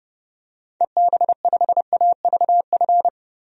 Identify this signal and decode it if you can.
Morse code, heard as a keyed tone, about 30 wpm, 700 Hz. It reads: EB5AVF
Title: DXCC-100 Call Signs - 6 Characters - Repeated 2x - 30wpm